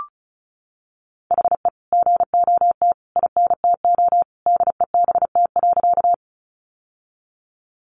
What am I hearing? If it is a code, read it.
Morse code, heard as a keyed tone, about 34 wpm, 700 Hz. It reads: HE GOT INTO DEBT.
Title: Sentences from Top 1300 Words 35wpm